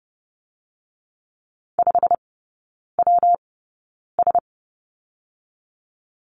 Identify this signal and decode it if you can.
Morse code, heard as a keyed tone, about 30 wpm, 700 Hz. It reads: 5 W S
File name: Lesson 19e - Letter w - Three Characters 30wpm